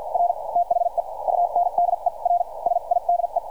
Morse code, heard as a keyed tone, about 50 wpm, spaced as for 39 wpm, 700 Hz. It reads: R THE UNDERSIDE